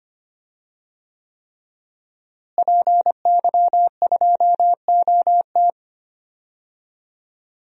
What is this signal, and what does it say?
Morse code, heard as a keyed tone, about 25 wpm, 700 Hz. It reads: PY2OT
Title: DXCC-100 Call Signs - 5 Characters - Repeated 3x - 25wpm